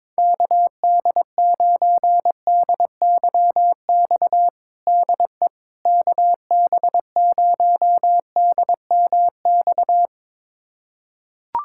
Morse code, heard as a keyed tone, about 22 wpm, 700 Hz. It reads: KD9DYX DE KB0DMX
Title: USA Callsign DE Callsign 22wpm